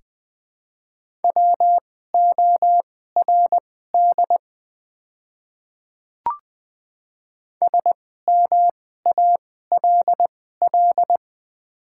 Morse code, heard as a keyed tone, about 21 wpm, spaced as for 15 wpm, 700 Hz. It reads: WORD SMALL
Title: 2x Character Spacing - Top 300 Words - 20wpm